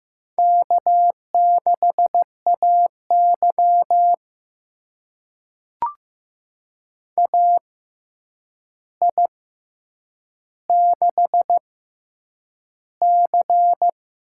Morse code, heard as a keyed tone, about 15 wpm, spaced as for 8 wpm, 700 Hz. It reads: K6AY AI6C
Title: USA Call Signs - 6x Character Spacing - 15wpm